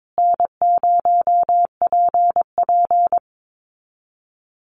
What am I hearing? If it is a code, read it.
Morse code, heard as a keyed tone, about 22 wpm, 700 Hz. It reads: N0PP
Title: Easy Call Signs Repeated 2x - 22wpm